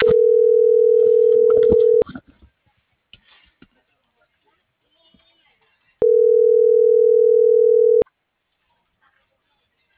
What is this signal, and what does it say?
Ringback tone (North America), 440 + 480 Hz.